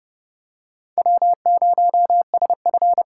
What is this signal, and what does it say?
Morse code, heard as a keyed tone, about 30 wpm, 700 Hz. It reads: W0SF
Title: USA Call Signs - 6x Character Spacing - 30wpm